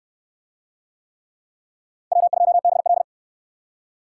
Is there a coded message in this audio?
Yes: Morse code from 2.1 to 3.0 s, heard as a keyed tone, about 66 wpm, 700 Hz. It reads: W2BL